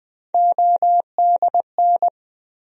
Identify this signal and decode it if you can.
Morse code, heard as a keyed tone, about 20 wpm, 700 Hz. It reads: ODN
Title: USA Call Sign Postfixes - 3 Letters 20wpm